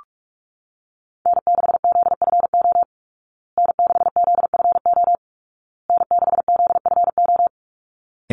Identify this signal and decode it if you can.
Morse code, heard as a keyed tone, about 44 wpm, 700 Hz. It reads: N6ZFO N6ZFO N6ZFO
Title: Call Signs Repeated 2x - 45wpm